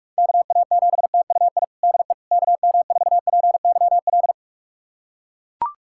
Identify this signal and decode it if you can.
Morse code, heard as a keyed tone, about 41 wpm, 690 Hz. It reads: KA7TUI DE KM4PYL
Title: USA Callsign DE Callsign 45wpm